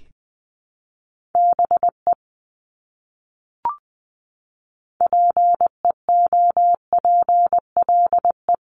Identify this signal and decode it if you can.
Morse code, heard as a keyed tone, about 20 wpm, 700 Hz. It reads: BE PEOPLE
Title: Top 300 Words Repeated 20wpm